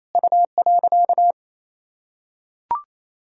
Morse code, heard as a keyed tone, about 28 wpm, 700 Hz. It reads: U.
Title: Sentences from Top 200 Words and Abbreviations 28wpm